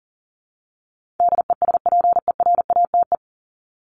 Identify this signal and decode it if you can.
Morse code, heard as a keyed tone, about 40 wpm, 700 Hz. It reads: DESPERATE
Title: Top 1751-2000 Words 40wpm